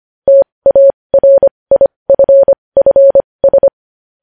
Morse code, heard as a keyed tone, about 25 wpm, spaced as for 20 wpm, 550 Hz. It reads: TARIFFS
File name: podcast / News headlines-1x-25x20